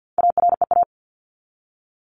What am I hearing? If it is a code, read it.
Morse code, heard as a keyed tone, about 50 wpm, 700 Hz. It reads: AREA
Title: ES:S Suffix - Top 100 Words with Base Words 50wpm